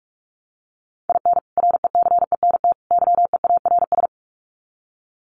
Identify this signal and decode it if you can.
Morse code, heard as a keyed tone, about 44 wpm, 700 Hz. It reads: IN RECENT YEARS
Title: Sets of 3 Words 45wpm